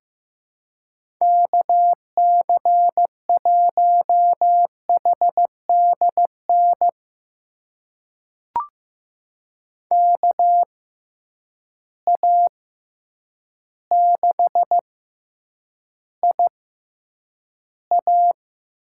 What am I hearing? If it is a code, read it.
Morse code, heard as a keyed tone, about 15 wpm, spaced as for 8 wpm, 700 Hz. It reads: KC1HDN KA6IA
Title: USA Call Signs - 6 Characters - 6x Character Spacing - 15wpm